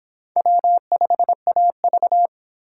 Morse code, heard as a keyed tone, about 26 wpm, 700 Hz. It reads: W5AV